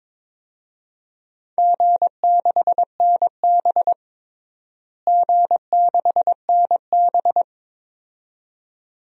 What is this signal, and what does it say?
Morse code, heard as a keyed tone, about 22 wpm, 700 Hz. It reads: G6NB G6NB
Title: DXCC-100 Call Signs - 4 Characters - Repeated 2x - 22wpm